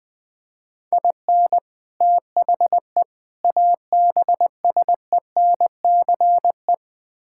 Morse code, heard as a keyed tone, about 20 wpm, 700 Hz. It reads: IN THE ABSENCE
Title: Build A Phrase - 4-Word Phrases - 20wpm